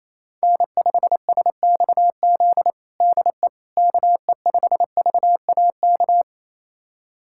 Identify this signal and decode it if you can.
Morse code, heard as a keyed tone, about 28 wpm, 700 Hz. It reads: N5SXZ DE KE5VAK
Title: USA Callsign DE Callsign 28wpm